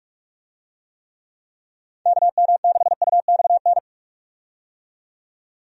Morse code, heard as a keyed tone, about 46 wpm, 700 Hz. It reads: KM6UXN
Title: USA Call Signs - 6 Characters - 45wpm